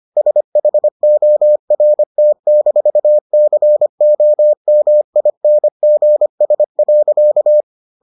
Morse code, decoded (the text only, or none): SHORT-COMINGS.